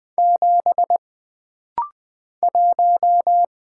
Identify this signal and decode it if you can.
Morse code, heard as a keyed tone, about 20 wpm, 700 Hz. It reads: 7 1
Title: Single Letter-Number - Rapid-Fire 20wpm